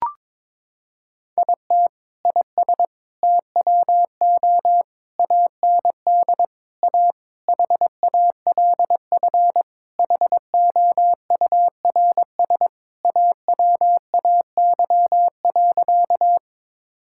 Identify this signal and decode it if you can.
Morse code, heard as a keyed tone, about 22 wpm, 700 Hz. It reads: IT IS TWO AND A HALF HOURS AWAY.
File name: practice / 22wpm / Sentences with 4-Word Phrases - 22wpm